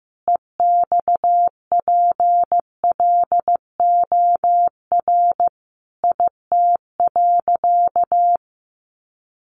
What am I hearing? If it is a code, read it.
Morse code, heard as a keyed tone, about 15 wpm, 700 Hz. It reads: EXPLOR IT.
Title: Sentences from Top 1100 Words 15wpm